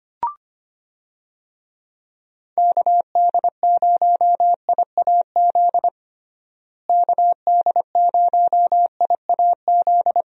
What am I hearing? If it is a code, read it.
Morse code, heard as a keyed tone, about 25 wpm, 700 Hz. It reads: KD0IAZ KD0IAZ
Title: USA Call Signs - 6 Characters - Repeated 2x - 25wpm